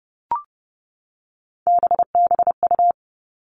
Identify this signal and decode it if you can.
Morse code, heard as a keyed tone, about 30 wpm, 700 Hz. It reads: BBU
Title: All Cut Numbers - 3-Digit 30wpm